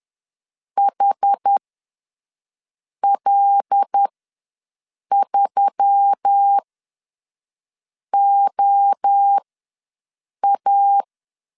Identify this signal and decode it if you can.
Morse code, heard as a keyed tone, about 10 wpm, spaced as for 5 wpm, 800 Hz. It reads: HL3OA
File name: LEZIONI / 5 LBVC / 5 L ws 10wpm 3636